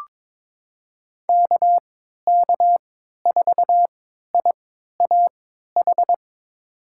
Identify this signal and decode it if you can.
Morse code, heard as a keyed tone, about 21 wpm, spaced as for 12 wpm, 700 Hz. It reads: KK4IAH
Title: USA Call Signs - 3x Character Spacing - 22wpm